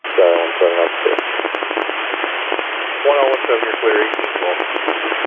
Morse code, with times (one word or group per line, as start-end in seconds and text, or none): none